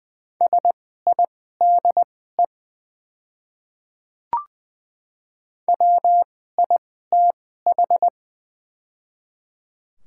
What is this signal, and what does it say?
Morse code, heard as a keyed tone, about 21 wpm, spaced as for 15 wpm, 700 Hz. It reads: SIDE WITH
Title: 2x Character Spacing - Top 100 Words - 20wpm